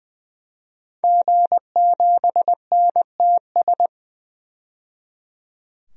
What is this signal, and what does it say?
Morse code, heard as a keyed tone, about 20 wpm, 700 Hz. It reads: G7NTS
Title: DXCC-100 Call Signs - 5 Characters - 20wpm